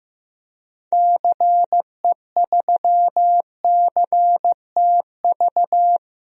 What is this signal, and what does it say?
Morse code, heard as a keyed tone, about 15 wpm, 700 Hz. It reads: CE3CTV
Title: DXCC-100 Call Signs - 6 Characters - 15wpm